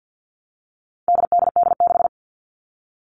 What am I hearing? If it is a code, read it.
Morse code, heard as a keyed tone, about 50 wpm, 700 Hz. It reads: DDD6
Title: All Cut Numbers - 4-Digit 50wpm